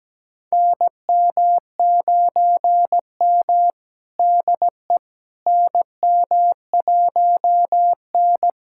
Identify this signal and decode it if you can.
Morse code, heard as a keyed tone, about 17 wpm, 700 Hz. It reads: NM9M DE NM1N
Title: USA Callsign DE Callsign - 4 Characters - 17wpm